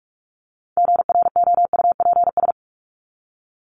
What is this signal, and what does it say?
Morse code, heard as a keyed tone, about 44 wpm, 700 Hz. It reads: GROUPS
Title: ES:S Suffix - Top 100 Words with Base Words 45wpm